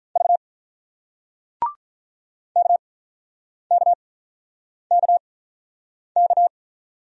Morse code, heard as a keyed tone, about 48 wpm, spaced as for 9 wpm, 700 Hz. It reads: V KKKK